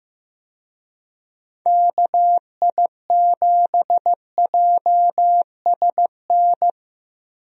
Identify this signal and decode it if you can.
Morse code, heard as a keyed tone, about 15 wpm, 700 Hz. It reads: KI7JSN
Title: USA Call Signs - 6 Characters - Repeated 3x - 15wpm